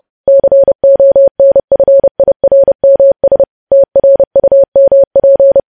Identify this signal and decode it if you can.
Morse code, heard as a keyed tone, about 30 wpm, 550 Hz. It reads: CONFIRMS TRUMP
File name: podcast / News headlines-1x-30x30